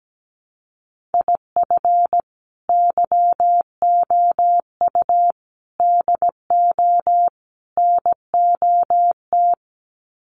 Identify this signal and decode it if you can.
Morse code, heard as a keyed tone, about 17 wpm, 700 Hz. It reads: IF YOU DO NOT
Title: Sets of 4 Words 17wpm